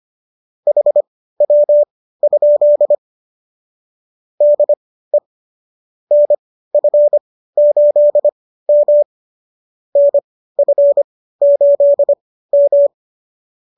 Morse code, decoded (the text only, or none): HW? DE NF8M NF8M